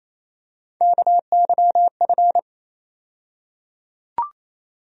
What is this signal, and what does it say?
Morse code, heard as a keyed tone, about 28 wpm, 700 Hz. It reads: KYF
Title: USA Call Sign Postfixes - 3 Letters 28wpm